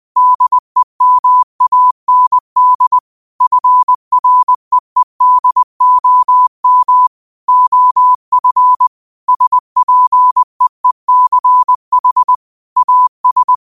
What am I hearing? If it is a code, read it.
Morse code, heard as a keyed tone, about 20 wpm, 1000 Hz. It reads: DEMAND FREEDOM OF SPEECH AS